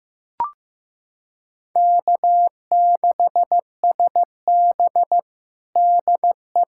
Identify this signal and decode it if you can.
Morse code, heard as a keyed tone, about 15 wpm, 700 Hz. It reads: K6SB DE
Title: USA Callsign DE Callsign - 4 Characters - 15wpm